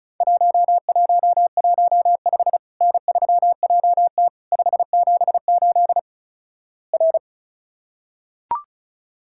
Morse code, heard as a keyed tone, about 36 wpm, 700 Hz. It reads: 1115 N3JT 578 R
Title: Worked All Europe DX Contest - QTC Exchange - 35wpm